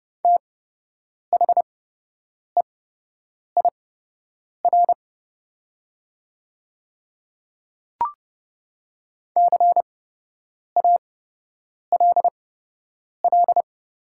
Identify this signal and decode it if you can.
Morse code, heard as a keyed tone, about 30 wpm, spaced as for 8 wpm, 700 Hz. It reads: THEIR CALL
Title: 8x Character Spacing - Top 100 Words - 30wpm